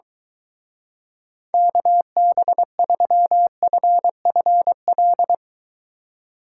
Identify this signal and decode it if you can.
Morse code, heard as a keyed tone, about 23 wpm, 700 Hz. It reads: KB3FFL